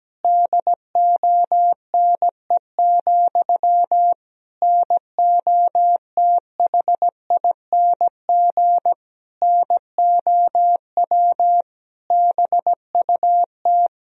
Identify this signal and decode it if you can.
Morse code, heard as a keyed tone, about 17 wpm, 700 Hz. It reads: DONE, NOTHING NOW BUT
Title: Sentences from Top 1000 Words - Review Part 4 - 17wpm